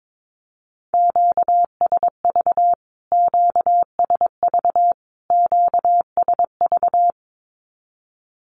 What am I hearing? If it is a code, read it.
Morse code, heard as a keyed tone, about 22 wpm, 700 Hz. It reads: QSV QSV QSV
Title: Q-Codes Repeated 3x 22wpm